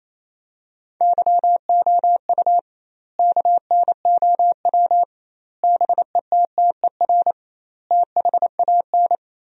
Morse code, heard as a keyed tone, about 28 wpm, 700 Hz. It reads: YOU KNOW BETTER THAN
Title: 2x Word-Spacing - Sentences from Top 400 Words - 28wpm